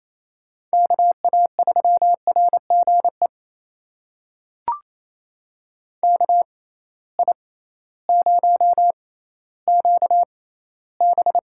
Morse code, heard as a keyed tone, about 28 wpm, spaced as for 15 wpm, 700 Hz. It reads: KA3RGE KI0QB
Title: USA Call Signs - 6x Character Spacing - 28wpm